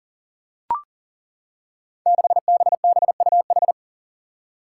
Morse code, heard as a keyed tone, about 40 wpm, 700 Hz. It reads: 6BBUH
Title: Five Challenging Letter-Number Combinations 40wpm